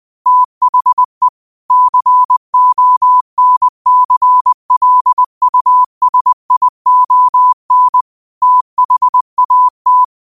Morse code, heard as a keyed tone, about 20 wpm, 1000 Hz. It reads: THE CONCLUSION THAT
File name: qotdfiles / 20wpm